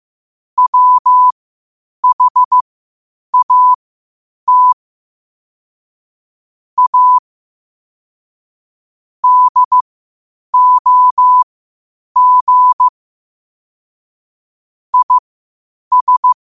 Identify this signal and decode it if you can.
Morse code, heard as a keyed tone, about 15 wpm, spaced as for 8 wpm, 1000 Hz. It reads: WHAT A DOG IS